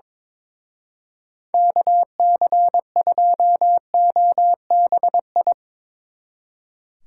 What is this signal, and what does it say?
Morse code, heard as a keyed tone, about 22 wpm, 700 Hz. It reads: KC2OBI
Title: Speed Racing - USA 2x3 Call Signs 22wpm